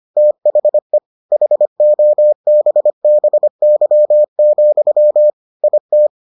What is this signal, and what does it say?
Morse code, heard as a keyed tone, about 25 wpm, 600 Hz. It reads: THE HOBBY, IT